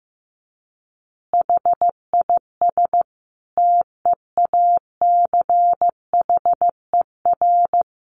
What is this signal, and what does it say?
Morse code, heard as a keyed tone, about 14 wpm, 700 Hz. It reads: HIS TEACHER
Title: Additional Easy Phrases 15wpm